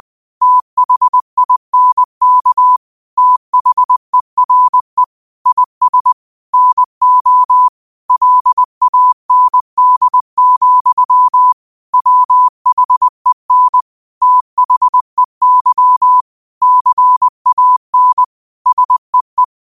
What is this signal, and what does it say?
Morse code, heard as a keyed tone, about 20 wpm, 1000 Hz. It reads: THINK THERE IS NO LAND, WHEN THEY CAN SEE
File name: qotdfiles / 20wpm